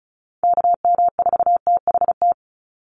Morse code, heard as a keyed tone, about 34 wpm, 700 Hz. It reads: KM4THT
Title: USA Call Signs 35wpm